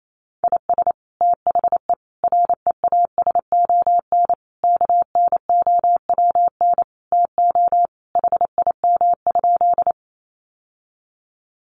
Morse code, heard as a keyed tone, about 28 wpm, 700 Hz. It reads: IS THE REASON KNOWN TO HIM?
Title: Sentences from Top 700 Words 28wpm